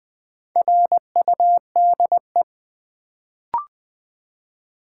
Morse code, heard as a keyed tone, about 20 wpm, 700 Hz. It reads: RUDE
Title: Build A Word - 1 Letter At A Time - 4-Letter Words - 20wpm